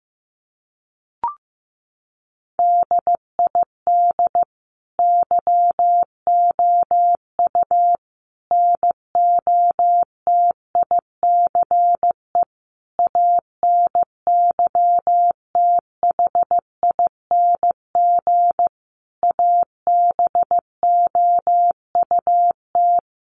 Morse code, heard as a keyed tone, about 15 wpm, 700 Hz. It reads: DID YOU NOTICE ANYTHING ABOUT
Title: Sentences from Top 1000 Words 15wpm